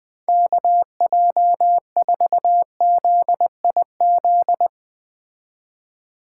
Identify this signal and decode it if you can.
Morse code, heard as a keyed tone, about 21 wpm, 700 Hz. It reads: KJ4ZIZ